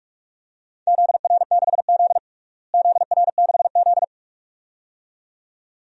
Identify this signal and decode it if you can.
Morse code, heard as a keyed tone, about 46 wpm, 700 Hz. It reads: ZR6Z ZR6Z
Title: DXCC-100 Call Signs - 4 Characters - Repeated 2x - 45wpm